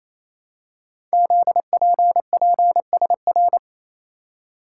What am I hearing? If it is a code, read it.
Morse code, heard as a keyed tone, about 27 wpm, 700 Hz. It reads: ZPPSR
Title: Code Groups - 5 Letters - 28wpm